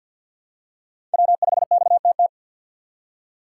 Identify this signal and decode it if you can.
Morse code, heard as a keyed tone, about 52 wpm, 700 Hz. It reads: W5XTT
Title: USA Call Signs - 5 Characters - 2x Character Spacing - 50wpm